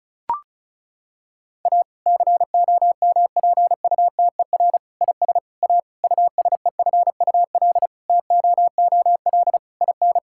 Morse code, heard as a keyed tone, about 36 wpm, 700 Hz. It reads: A COMPUTER IS A USEFUL TOOL IN
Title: Sentences from Top 1000 Words - Review Part 6 - 35wpm